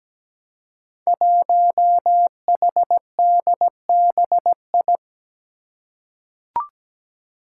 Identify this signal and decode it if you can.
Morse code, heard as a keyed tone, about 17 wpm, 700 Hz. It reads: 1HDBI
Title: Five Challenging Letter-Number Combinations 17wpm